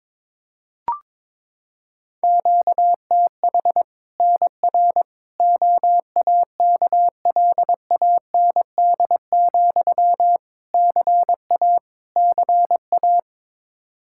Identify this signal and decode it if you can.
Morse code, heard as a keyed tone, about 21 wpm, 700 Hz. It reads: QTH NR OAKLAND, CA CA
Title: K1USN SST QSO Practice 22wpm